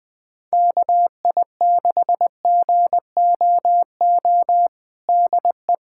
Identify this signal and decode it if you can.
Morse code, heard as a keyed tone, about 21 wpm, 700 Hz. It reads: KI6GOO DE
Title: USA Callsign DE Callsign - 6 Characters - 20wpm